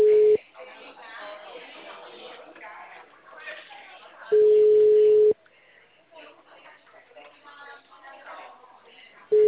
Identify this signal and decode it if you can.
Ringback tone (Europe, 425 Hz), with speech.